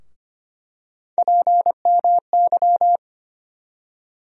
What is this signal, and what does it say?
Morse code, heard as a keyed tone, about 25 wpm, 700 Hz. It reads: PMY
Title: USA Call Sign Postfixes - 3 Letters 25wpm